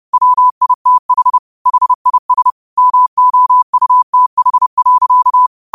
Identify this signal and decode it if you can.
Morse code, heard as a keyed tone, about 30 wpm, 1000 Hz. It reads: WITH HIS MOUTH.